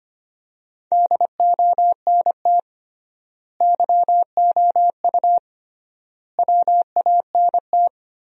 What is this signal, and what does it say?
Morse code, heard as a keyed tone, about 25 wpm, 700 Hz. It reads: DONT YOU WANT
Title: 3x Word-Spacing - Sentences from Top 300 Words - 25wpm